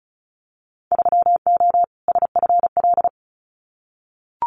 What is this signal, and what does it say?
Morse code, heard as a keyed tone, about 34 wpm, 700 Hz. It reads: 3O SFL
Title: Winter Field Day Exchanges 35wpm